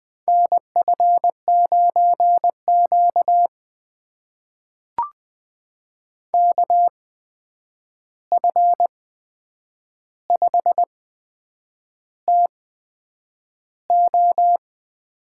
Morse code, heard as a keyed tone, about 20 wpm, spaced as for 8 wpm, 700 Hz. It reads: NF9Q KF5TO
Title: USA Call Signs - 8x Character Spacing - 20wpm